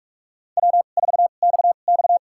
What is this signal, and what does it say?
Morse code, heard as a keyed tone, about 44 wpm, spaced as for 32 wpm, 700 Hz. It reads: W4XX
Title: USA Call Signs - 4 Characters - 2x Character Spacing - 45wpm